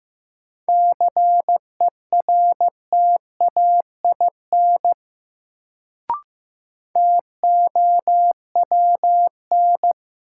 Morse code, heard as a keyed tone, about 15 wpm, 700 Hz. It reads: CERTAIN TOWN
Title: Sentences from Top 500 Words - New Word Drill - 15wpm